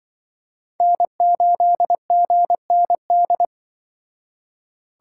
Morse code, heard as a keyed tone, about 23 wpm, 700 Hz. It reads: N8GND
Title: Speed Racing - USA 1x3 Call Signs 20wpm